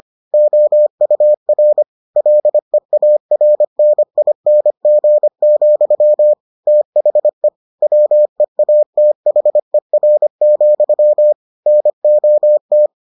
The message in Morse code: OUR LEARNING, THE WEATHER, NOT

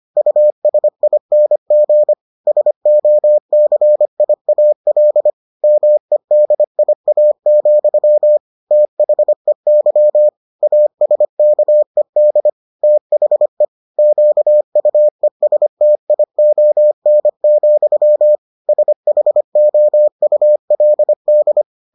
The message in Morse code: USING SOCIAL MEDIA, THEY ASKED THE QUESTION, SHOULD

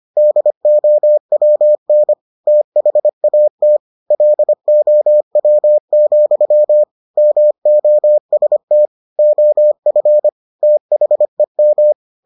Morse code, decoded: DOWN THAT LOW, MOST OF THEM

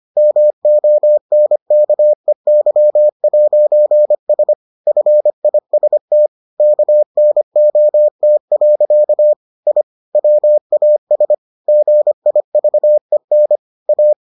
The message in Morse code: MONKEY'S FIST KNOT. I WAS GIVEN A

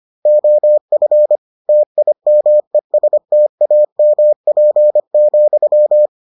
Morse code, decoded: OF TIMESTAMP,

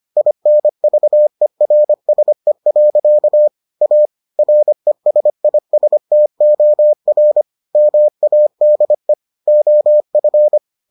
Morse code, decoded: INVERSE. A RESISTOR MADE OF

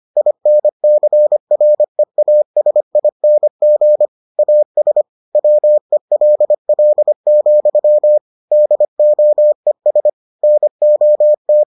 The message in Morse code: INCREASING AS WELL, DOES NOT

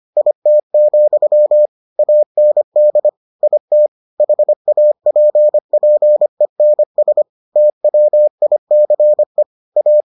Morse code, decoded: IT, AND IT HAPPENS TWICE A

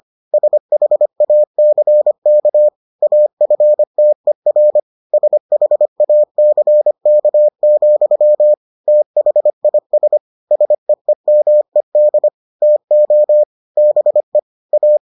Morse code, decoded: SHACK AFTER SHACK, THIS SEEMED TO BE A

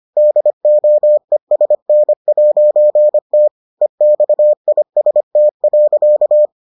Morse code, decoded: DOESN'T EXIST.

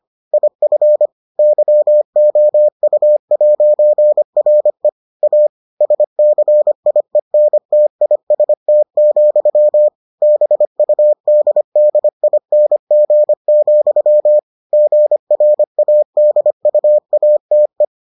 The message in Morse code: IF YOU'RE A SCIENTIST, BUDDING, GRADUATE

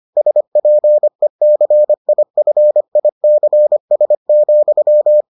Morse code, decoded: SPECIFICS,

